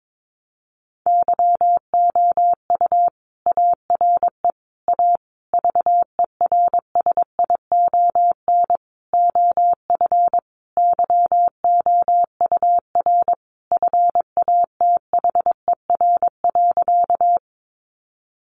Morse code, heard as a keyed tone, about 22 wpm, 700 Hz. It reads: YOU ARE A VERSION OF YOUR FATHER.
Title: Sentences from Top 1400 Words 22wpm